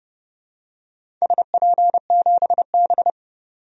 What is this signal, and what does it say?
Morse code, heard as a keyed tone, about 30 wpm, 700 Hz. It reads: SP7B
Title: DXCC-100 Call Signs - 4 Characters - 30wpm